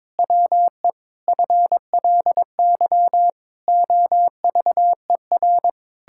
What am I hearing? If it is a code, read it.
Morse code, heard as a keyed tone, about 22 wpm, 700 Hz. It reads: WE FLY OVER
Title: Pangram Challenge v2 - 22wpm